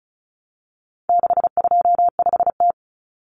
Morse code, heard as a keyed tone, about 34 wpm, 700 Hz. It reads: 625T
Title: Common Cut Numbers - 4-Digit 35wpm